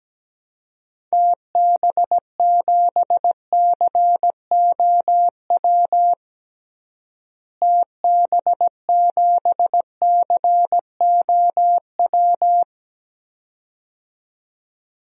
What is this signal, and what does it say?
Morse code, heard as a keyed tone, about 17 wpm, 700 Hz. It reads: TB7COW TB7COW
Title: DXCC-100 Call Signs - 6 Characters - Repeated 2x - 17wpm